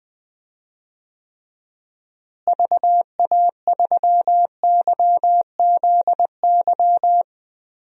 Morse code, heard as a keyed tone, about 21 wpm, 700 Hz. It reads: VA3YZY
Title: DXCC-100 Call Signs - 6 Characters - Repeated 2x - 20wpm